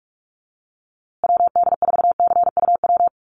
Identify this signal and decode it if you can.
Morse code, heard as a keyed tone, about 44 wpm, 690 Hz. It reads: WD4CUW
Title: USA Call Signs Repeated 3x 45wpm